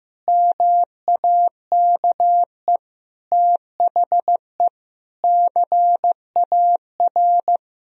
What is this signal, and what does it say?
Morse code, heard as a keyed tone, about 15 wpm, 700 Hz. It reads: MAKE THE CAR
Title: Sentences from Top 1000 Words - Review Part 1 - 15wpm